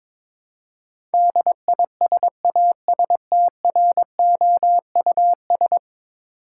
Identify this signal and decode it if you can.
Morse code, heard as a keyed tone, about 22 wpm, 700 Hz. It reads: DISASTROUS
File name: practice / 22wpm / 10 Letter Words - First 3 Letters Given - 22wpm